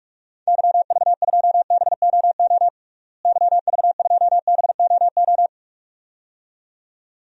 Morse code, heard as a keyed tone, about 42 wpm, 700 Hz. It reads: YV2BOO YV2BOO
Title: DXCC-100 Call Signs - 6 Characters - Repeated 2x - 45wpm